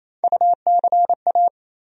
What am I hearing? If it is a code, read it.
Morse code, heard as a keyed tone, about 28 wpm, 700 Hz. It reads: UCA
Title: USA Call Sign Postfixes - 3 Letters 28wpm